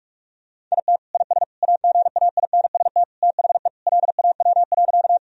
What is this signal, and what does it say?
Morse code, heard as a keyed tone, about 46 wpm, 700 Hz. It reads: IT IS AGAINST THE LAW.
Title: Sentences from Top 1000 Words - Review Part 3 - 45wpm